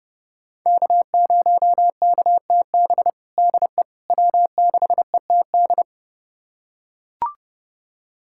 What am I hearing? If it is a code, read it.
Morse code, heard as a keyed tone, about 32 wpm, 700 Hz. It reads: K0KTB DE W6ETD
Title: USA Callsign DE Callsign - 5 Characters - 30wpm